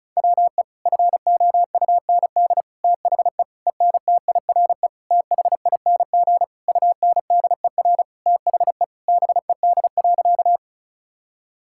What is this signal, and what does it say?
Morse code, heard as a keyed tone, about 34 wpm, 700 Hz. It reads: WE FOUND THE ENTIRE THING UNDER THE BED.
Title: Sentences from Top 1000 Words - Review Part 5 - 35wpm